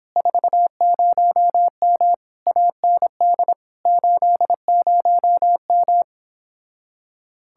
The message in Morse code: 40M AND 80M

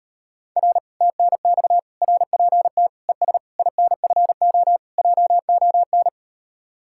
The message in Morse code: R TNX RPT ES INFO JON

